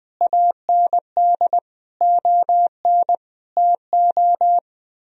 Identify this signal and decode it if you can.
Morse code, heard as a keyed tone, about 20 wpm, 700 Hz. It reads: AND ON TO
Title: Sentences from Top 1000 Words - Review Part 2 - 20wpm